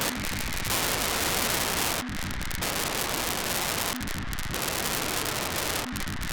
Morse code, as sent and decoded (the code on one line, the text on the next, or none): none